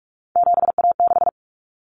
Morse code, heard as a keyed tone, about 44 wpm, 700 Hz. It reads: 7A6
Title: Common Cut Numbers - 3-Digit 45wpm